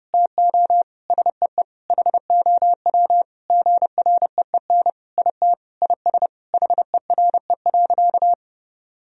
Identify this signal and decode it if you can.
Morse code, heard as a keyed tone, about 30 wpm, 700 Hz. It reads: TO SEE HOW GREEN IT IS HERE.
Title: Sentences from Top 1000 Words - Review Part 2 - 30wpm